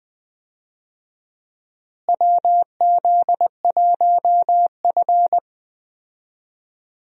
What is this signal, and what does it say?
Morse code, heard as a keyed tone, about 20 wpm, 700 Hz. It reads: WZ1F
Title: USA Call Signs - 4 Characters - Repeated 3x - 20wpm